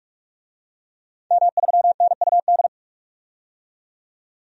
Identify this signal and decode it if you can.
Morse code, heard as a keyed tone, about 42 wpm, 700 Hz. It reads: M3NUD
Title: DXCC-100 Call Signs - 5 Characters - 45wpm